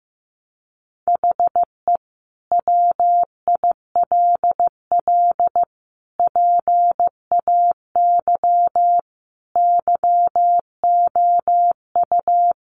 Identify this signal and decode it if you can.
Morse code, heard as a keyed tone, about 15 wpm, 700 Hz. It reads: HE WILL PAY YOU
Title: Sentences from Top 1400 Words 15wpm